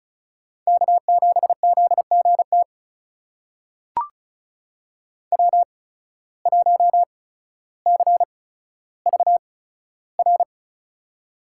Morse code, heard as a keyed tone, about 36 wpm, spaced as for 15 wpm, 700 Hz. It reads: K7ZGT W1CVR
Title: USA Call Signs - 5 Characters - 8x Character Spacing - 35wpm